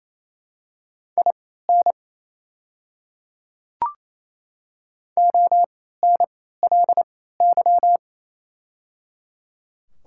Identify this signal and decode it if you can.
Morse code, heard as a keyed tone, about 27 wpm, spaced as for 16 wpm, 690 Hz. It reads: IN ONLY
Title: 3x Character Spacing - Top 200 Words - 28wpm